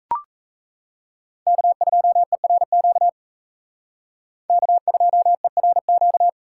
Morse code, heard as a keyed tone, about 35 wpm, 700 Hz. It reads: K2ERQ K2ERQ